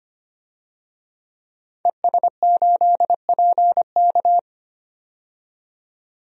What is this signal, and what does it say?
Morse code, heard as a keyed tone, about 25 wpm, 700 Hz. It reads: ES8PK
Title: DXCC-100 Call Signs - 5 Characters - 25wpm